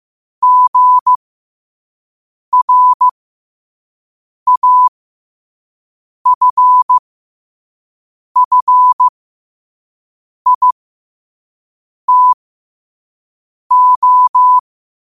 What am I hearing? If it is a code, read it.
Morse code, heard as a keyed tone, about 15 wpm, spaced as for 5 wpm, 1000 Hz. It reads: GRAFFITO
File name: qotdfiles / 5wpm